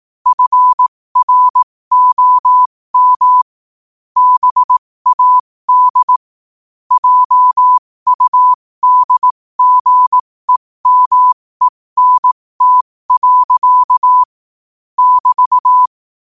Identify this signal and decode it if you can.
Morse code, heard as a keyed tone, about 18 wpm, spaced as for 15 wpm, 1000 Hz. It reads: FROM BAD JUDGEMENT. <BT>
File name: qotdfiles / 15wpm